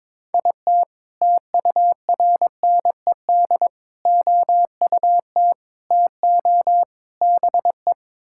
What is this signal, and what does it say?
Morse code, heard as a keyed tone, about 22 wpm, 700 Hz. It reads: IT TURNED OUT TO BE
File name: practice / 22wpm / Sentences with 4-Word Phrases - 22wpm